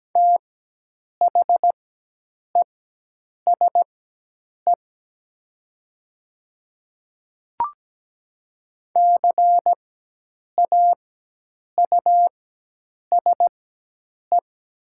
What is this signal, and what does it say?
Morse code, heard as a keyed tone, about 17 wpm, spaced as for 8 wpm, 700 Hz. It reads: THESE CAUSE
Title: 4x Character Spacing - Top 300 Words - 17wpm